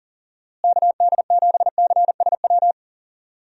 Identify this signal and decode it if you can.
Morse code, heard as a keyed tone, about 37 wpm, 700 Hz. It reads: KD7CSW
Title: USA Call Signs - 6 Characters - Repeated 3x - 40wpm